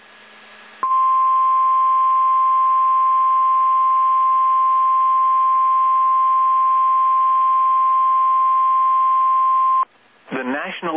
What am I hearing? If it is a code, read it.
Emergency Alert System (SAME) tones (the 1050 Hz attention tone), with speech. It is heard from 0.8 to 9.9 s.